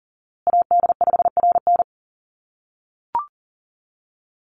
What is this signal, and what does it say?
Morse code, heard as a keyed tone, about 40 wpm, 700 Hz. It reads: AD5RN
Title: USA Call Signs Repeated 2x 40wpm